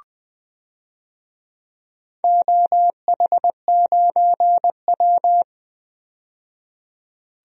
Morse code, heard as a keyed tone, about 20 wpm, 700 Hz. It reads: OH9W
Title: DXCC-100 Call Signs - 4 Characters - 20wpm